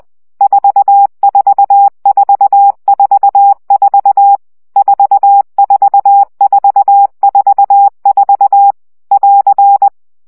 Morse code, heard as a keyed tone, about 20 wpm, 800 Hz. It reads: 44444 44444 <AR>
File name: CW gyakorlas 4 17 baud